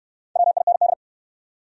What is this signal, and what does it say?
Morse code, heard as a keyed tone, about 69 wpm, 700 Hz. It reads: WIND